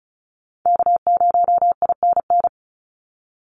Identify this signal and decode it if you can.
Morse code, heard as a keyed tone, about 36 wpm, 700 Hz. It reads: K0INN
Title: Call Signs - 35wpm